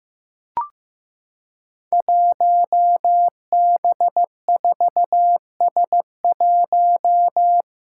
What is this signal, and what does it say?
Morse code, heard as a keyed tone, about 15 wpm, 700 Hz. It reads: 1B4S1
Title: Five Challenging Letter-Number Combinations 15wpm